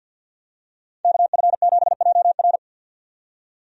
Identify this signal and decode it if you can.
Morse code, heard as a keyed tone, about 46 wpm, 700 Hz. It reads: KF7JR
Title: USA Call Signs - 5 Characters - 50wpm